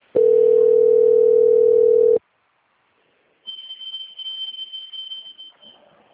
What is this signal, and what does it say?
Ringback tone (North America), 440 + 480 Hz.
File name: out- - - - - .11841